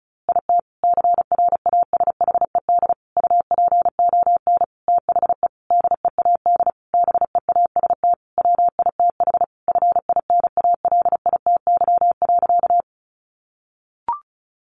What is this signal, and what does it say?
Morse code, heard as a keyed tone, about 34 wpm, 700 Hz. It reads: IT CRASHED UPON THE DEAD BEAST WITH FINALITY.
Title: A Sound Of Thunder 35wpm